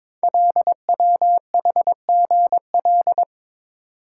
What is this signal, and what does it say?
Morse code, heard as a keyed tone, about 21 wpm, 700 Hz. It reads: LWHGL
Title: Code Groups - 5 Letter:Numbers - 22wpm